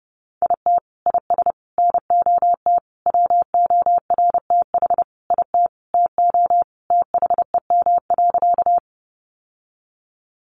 Morse code, heard as a keyed tone, about 30 wpm, 700 Hz. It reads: IT IS NOT WORTH IT TO THEM.
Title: Sentences from Top 800 Words 30wpm